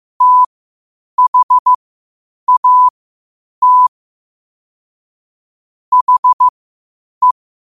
Morse code, heard as a keyed tone, about 15 wpm, spaced as for 8 wpm, 1000 Hz. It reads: THAT HE
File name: qotdfiles / 8wpm